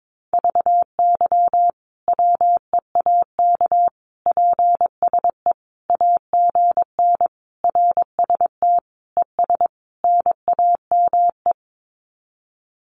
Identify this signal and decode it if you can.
Morse code, heard as a keyed tone, about 22 wpm, 700 Hz. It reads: VY WEAK PSE AGN RST ES NAME
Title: Common QSO Phrases 22wpm